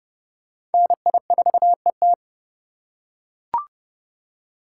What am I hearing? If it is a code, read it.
Morse code, heard as a keyed tone, about 28 wpm, 700 Hz. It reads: NI4ET